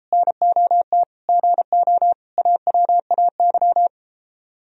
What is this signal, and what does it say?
Morse code, heard as a keyed tone, about 33 wpm, 700 Hz. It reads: NOT GO AWAY